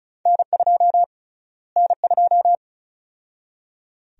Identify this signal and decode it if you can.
Morse code, heard as a keyed tone, about 36 wpm, 700 Hz. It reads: N2 N2
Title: USA Call Sign Prefixes - Repeated 2x - 35wpm